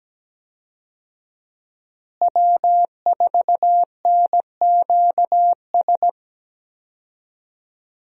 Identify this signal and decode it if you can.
Morse code, heard as a keyed tone, about 17 wpm, 700 Hz. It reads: W4NQS
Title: USA Call Signs - 5 Characters - Repeated 2x - 17wpm